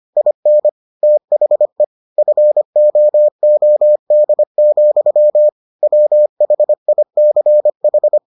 Morse code, heard as a keyed tone, about 25 wpm, 600 Hz. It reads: IN THE FOOD, WHICH